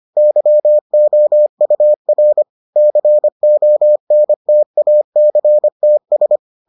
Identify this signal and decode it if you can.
Morse code, heard as a keyed tone, about 25 wpm, 600 Hz. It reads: YOUR CONTACTS